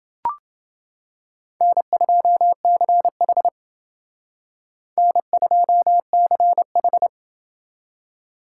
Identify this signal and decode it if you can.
Morse code, heard as a keyed tone, about 28 wpm, 700 Hz. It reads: N2CH N2CH